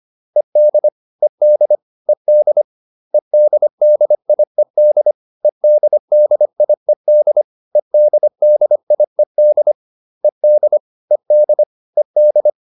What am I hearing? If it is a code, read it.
Morse code, heard as a keyed tone, about 25 wpm, 600 Hz. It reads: ED ED ED EDDIED EDDIED EDDIED ED ED ED